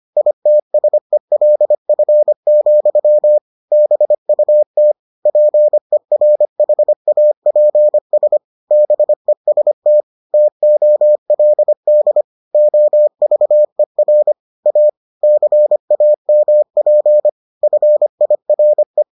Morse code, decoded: ITSELF, BUT PERHAPS BEST TOLD OVER A CAMP FIRE

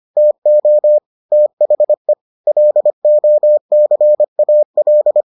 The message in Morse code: TO THE LOCAL

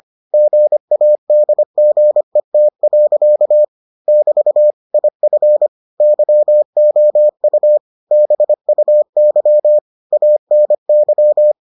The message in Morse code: GADGET. <BT> IF YOU BUY ANY

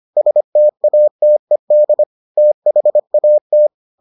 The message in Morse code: STATED THAT

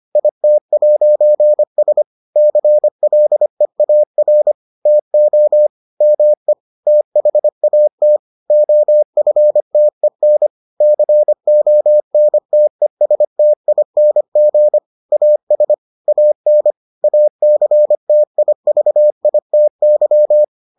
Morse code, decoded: IT'S CLEAR TO ME THAT OFTEN CONTESTING AS AN ACTIVITY

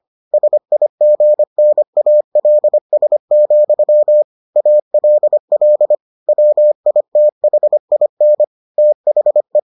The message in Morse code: SIGNALS, ALL WITHIN THE

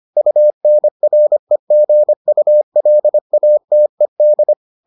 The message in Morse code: UNREGULATED

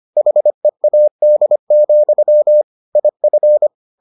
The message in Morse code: HEAD, IF

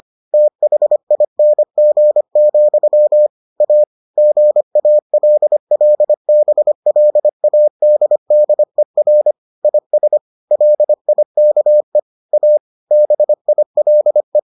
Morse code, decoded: THING, A GALLBLADDER IS LIKE A BILE

